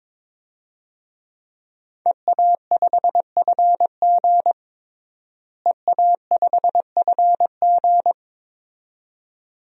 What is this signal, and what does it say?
Morse code, heard as a keyed tone, about 22 wpm, 700 Hz. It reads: EA5FG EA5FG
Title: DXCC-100 Call Signs - 5 Characters - Repeated 2x - 22wpm